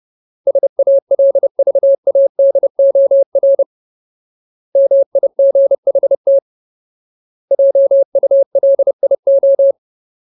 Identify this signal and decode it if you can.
Morse code, heard as a keyed tone, about 30 wpm, 550 Hz. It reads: SALVADOR MIGHT JULIO
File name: Ultimate Word List / 30WPM Ultimate